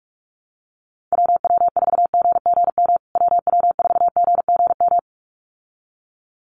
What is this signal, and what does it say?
Morse code, heard as a keyed tone, about 44 wpm, 700 Hz. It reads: WW4GGM WW4GGM
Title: USA Call Signs Repeated 2x 45wpm